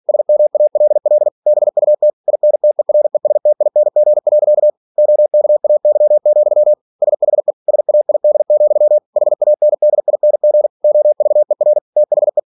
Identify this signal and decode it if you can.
Morse code, heard as a keyed tone, about 46 wpm, 600 Hz. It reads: SMALL BUT INTERESTING. OKAY, SHE SAID, HANDING OVER THE
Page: Never 47wpm